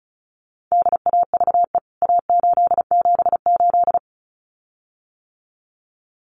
Morse code, heard as a keyed tone, about 34 wpm, 700 Hz. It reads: DAVE A878
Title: CWT Exchange 35wpm